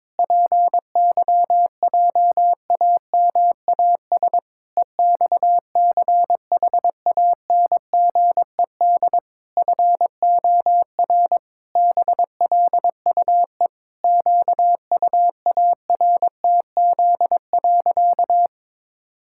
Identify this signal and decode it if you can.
Morse code, heard as a keyed tone, about 22 wpm, 700 Hz. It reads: PYJAMAS EXCHANGED FOR BLUE QUARTZ.
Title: Pangram Challenge v2 - 22wpm